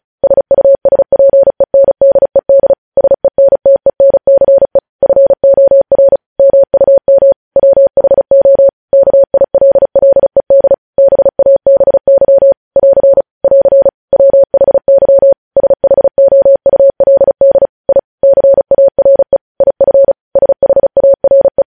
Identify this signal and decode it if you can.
Morse code, heard as a keyed tone, about 36 wpm, 550 Hz. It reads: SUSPENDED SENTENCE FOR MUM WHO KILLED BABY <AR> <AR> WHY SHOULD I CARE IF SHARE
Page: news headlines in morse code at 35 wpm